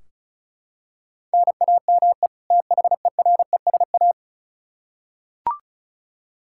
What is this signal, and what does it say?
Morse code, heard as a keyed tone, about 36 wpm, 700 Hz. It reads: NAME THERESA
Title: K1USN SST QSO Practice 35wpm